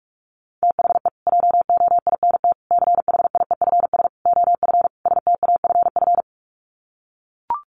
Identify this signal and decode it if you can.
Morse code, heard as a keyed tone, about 44 wpm, 700 Hz. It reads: THE JOINT CHIEFS OF STAFF
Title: Sets of 5 Words Encore2 45wpm